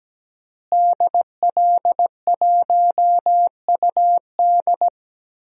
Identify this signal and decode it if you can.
Morse code, heard as a keyed tone, about 17 wpm, 700 Hz. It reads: DL1UD
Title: DXCC-100 Call Signs - 5 Characters - Repeated 3x - 17wpm